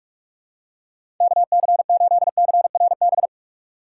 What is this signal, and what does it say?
Morse code, heard as a keyed tone, about 41 wpm, 700 Hz. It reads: KC8CRB
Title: USA Call Signs - 6 Characters - 2x Character Spacing - 45wpm